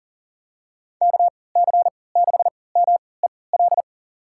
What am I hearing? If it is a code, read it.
Morse code, heard as a keyed tone, about 40 wpm, spaced as for 23 wpm, 700 Hz. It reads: KC6MEL
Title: USA Call Signs - 6 Characters - 3x Character Spacing - 40wpm